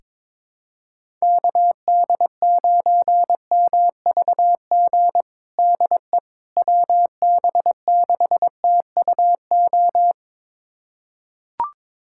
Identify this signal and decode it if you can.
Morse code, heard as a keyed tone, about 22 wpm, 700 Hz. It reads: KD9MVG DE WB6TUO
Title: USA Callsign DE Callsign 22wpm